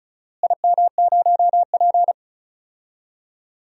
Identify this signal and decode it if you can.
Morse code, heard as a keyed tone, about 36 wpm, 700 Hz. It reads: IM0P
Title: DXCC-100 Call Signs - 4 Characters - Repeated 3x - 35wpm